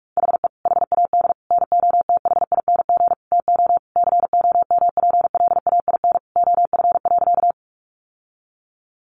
Morse code, heard as a keyed tone, about 44 wpm, 700 Hz. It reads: HE HAD NOTHING TO COMPLAIN OF.
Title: Sentences from Top 1300 Words 45wpm